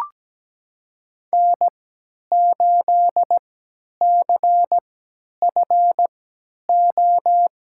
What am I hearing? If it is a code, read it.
Morse code, heard as a keyed tone, about 17 wpm, spaced as for 10 wpm, 700 Hz. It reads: N8CFO
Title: USA Call Signs - 5 Characters - 3x Character Spacing - 17wpm